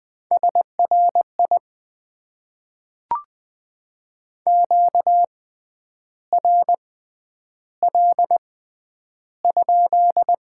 Morse code, heard as a keyed tone, about 20 wpm, spaced as for 9 wpm, 700 Hz. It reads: SRI QRL?
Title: Top 90 QSO Elements - 6x Character Spacing - 20wpm